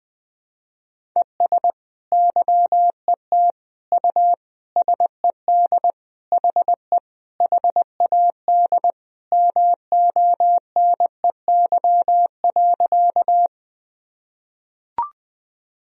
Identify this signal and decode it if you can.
Morse code, heard as a keyed tone, about 20 wpm, 700 Hz. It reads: ES YET U SED HE HAD MONEY.
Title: Sentences from Top 500 Words and Abbreviations 20wpm